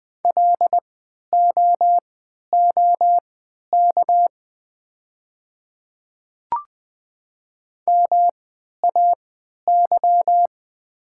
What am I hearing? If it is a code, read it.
Morse code, heard as a keyed tone, about 20 wpm, spaced as for 11 wpm, 700 Hz. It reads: LOOK MAY
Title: 3x Character Spacing - Top 100 Words - 20wpm